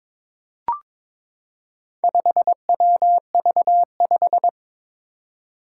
Morse code, heard as a keyed tone, about 22 wpm, 700 Hz. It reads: 5WV5
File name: practice / 22wpm / Four Letter-Number Combinations - 22wpm